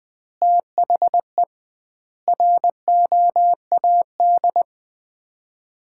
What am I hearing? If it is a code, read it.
Morse code, heard as a keyed tone, about 21 wpm, 700 Hz. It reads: THE ROAD
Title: 2x Word-Spacing - Sets of 5 Words - 20wpm